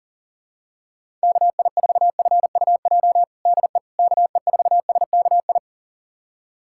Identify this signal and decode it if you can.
Morse code, heard as a keyed tone, about 40 wpm, 700 Hz. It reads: KI4FUJ DE KE4SKI
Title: USA Callsign DE Callsign - 6 Characters - 40wpm